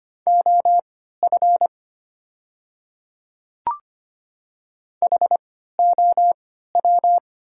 Morse code, heard as a keyed tone, about 25 wpm, spaced as for 14 wpm, 700 Hz. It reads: OF HOW
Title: 3x Character Spacing - Top 100 Words - 25wpm